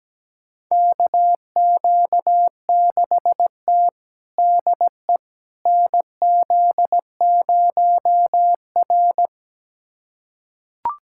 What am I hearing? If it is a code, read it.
Morse code, heard as a keyed tone, about 17 wpm, 700 Hz. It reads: KQ6T DE NZ0R
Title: USA Callsign DE Callsign - 4 Characters - 17wpm